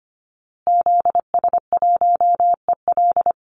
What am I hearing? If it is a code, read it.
Morse code, heard as a keyed tone, about 25 wpm, 700 Hz. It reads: ZS1EL
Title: Call Signs Repeated 2x - 25wpm